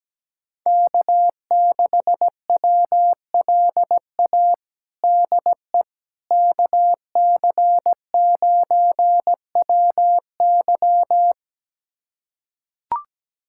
Morse code, heard as a keyed tone, about 17 wpm, 700 Hz. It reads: K6WLA DE KC9WY
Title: USA Callsign DE Callsign 17wpm